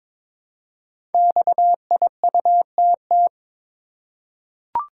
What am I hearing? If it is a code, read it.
Morse code, heard as a keyed tone, about 23 wpm, 700 Hz. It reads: XIUTT
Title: Code Groups - 5 Letters - 22wpm